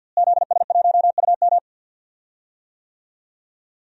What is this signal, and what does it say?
Morse code, heard as a keyed tone, about 51 wpm, 700 Hz. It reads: ZS1UM
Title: DXCC-100 Call Signs - 5 Characters - Repeated 3x - 50wpm